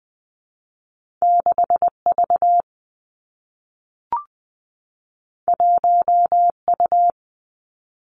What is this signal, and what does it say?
Morse code, heard as a keyed tone, about 20 wpm, 700 Hz. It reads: 6 V 1 U
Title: Two Challenging Letter-Number Combinations 20wpm